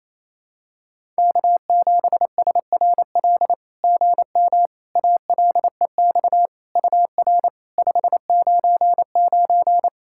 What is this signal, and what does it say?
Morse code, heard as a keyed tone, about 28 wpm, 700 Hz. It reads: K7SRL GM ALEX UR 599
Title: SOTA QSO Practice 28wpm